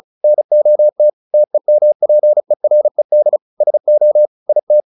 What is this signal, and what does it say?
Morse code, heard as a keyed tone, about 34 wpm, 600 Hz. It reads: NOT TEMPERED SO IT
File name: the old man and the sea 2 35wpm